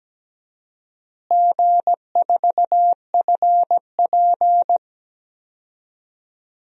Morse code, heard as a keyed tone, about 17 wpm, 700 Hz. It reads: G4FP
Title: DXCC-100 Call Signs - 4 Characters - Repeated 3x - 17wpm